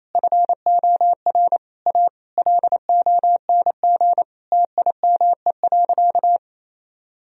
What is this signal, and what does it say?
Morse code, heard as a keyed tone, about 28 wpm, 700 Hz. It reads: FOR A LONG TIME.
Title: Sentences from Top 1000 Words - Review Part 4 - 28wpm